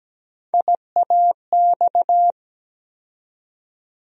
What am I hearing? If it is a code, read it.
Morse code, heard as a keyed tone, about 17 wpm, 700 Hz. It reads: IAX
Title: USA Call Sign Postfixes - 3 Letters 17wpm